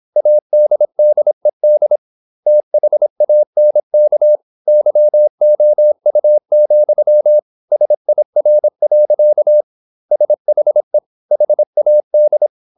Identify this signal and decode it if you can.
Morse code, heard as a keyed tone, about 26 wpm, 600 Hz. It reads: ADDED THANK YOU, SIR. SHE HAD
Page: Never 26wpm | CW med Gnister